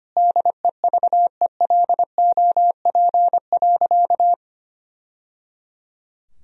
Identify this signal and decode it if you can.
Morse code, heard as a keyed tone, about 25 wpm, 700 Hz. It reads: DEVELOP.
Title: Sentences from Top 1000 Words - Review Part 2 - 25wpm